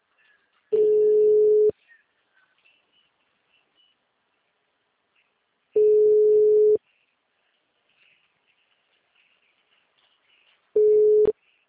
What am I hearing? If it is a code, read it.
Ringback tone (Europe, 425 Hz).